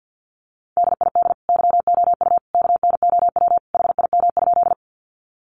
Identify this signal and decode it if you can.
Morse code, heard as a keyed tone, about 50 wpm, 700 Hz. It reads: DID YOU KNOW HIM?
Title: Sentences from Top 100 Words 50wpm